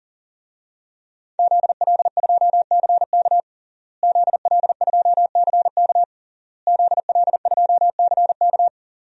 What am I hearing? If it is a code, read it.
Morse code, heard as a keyed tone, about 40 wpm, 700 Hz. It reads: ZL2CK ZL2CK ZL2CK
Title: DXCC-100 Call Signs - 5 Characters - Repeated 3x - 40wpm